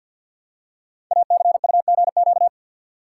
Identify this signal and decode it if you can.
Morse code, heard as a keyed tone, about 48 wpm, 700 Hz. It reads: ACUGQ
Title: Code Groups - 5 Letters - 50wpm